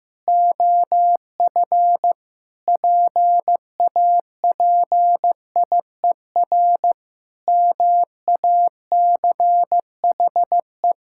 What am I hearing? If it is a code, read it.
Morse code, heard as a keyed tone, about 15 wpm, 700 Hz. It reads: OF PAPIER MACHE
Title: Pangram Challenge v2 - 15wpm